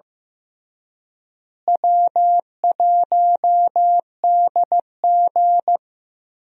Morse code, heard as a keyed tone, about 15 wpm, 700 Hz. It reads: W1DG
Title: USA Call Signs - 4 Characters - Repeated 3x - 15wpm